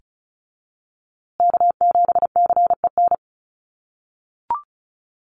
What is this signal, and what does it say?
Morse code, heard as a keyed tone, about 34 wpm, 700 Hz. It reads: K7CEN
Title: USA Call Signs Repeated 3x 35wpm